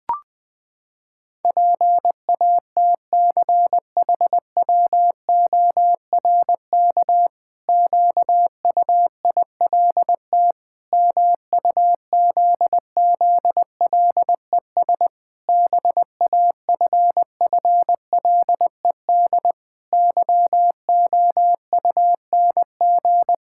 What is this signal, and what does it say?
Morse code, heard as a keyed tone, about 20 wpm, 700 Hz. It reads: PATCHWORK QUILT MUZZLES BAFFLED YOUNG
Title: Pangram Challenge v2 - 20wpm